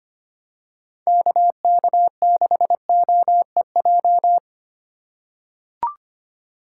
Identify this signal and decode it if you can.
Morse code, heard as a keyed tone, about 25 wpm, 700 Hz. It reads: KK6OEJ
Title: USA Call Signs - 4x Character Spacing - 25wpm